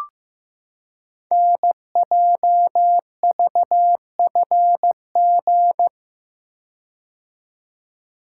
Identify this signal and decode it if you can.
Morse code, heard as a keyed tone, about 15 wpm, 690 Hz. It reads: NJVFG
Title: Code Groups - 5 Letters - 15wpm